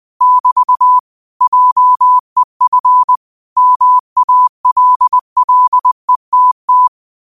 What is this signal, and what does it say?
Morse code, heard as a keyed tone, about 20 wpm, 1000 Hz. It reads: <BT> JEF MALLETT